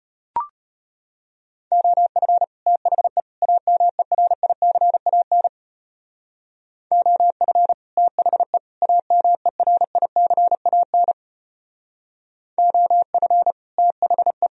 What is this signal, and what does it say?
Morse code, heard as a keyed tone, about 32 wpm, 700 Hz. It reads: OF THE AMERICAN OF THE AMERICAN OF THE